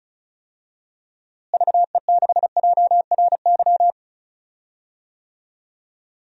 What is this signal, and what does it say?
Morse code, heard as a keyed tone, about 36 wpm, 700 Hz. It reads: VE6JRY
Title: DXCC-100 Call Signs - 6 Characters - Repeated 3x - 35wpm